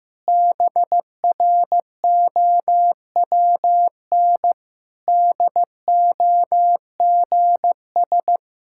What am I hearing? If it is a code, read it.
Morse code, heard as a keyed tone, about 15 wpm, 700 Hz. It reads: BROWN DOGS
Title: Pangram Challenge v2 - 15wpm